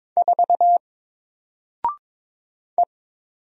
Morse code, heard as a keyed tone, about 22 wpm, 700 Hz. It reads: 4 E
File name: Single-Letter-Number-22wpm